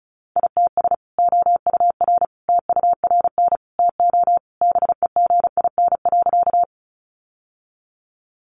Morse code, heard as a keyed tone, about 34 wpm, 700 Hz. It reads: ITS OUR TURN TO BEGIN.
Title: Sentences from Top 600 Words 35wpm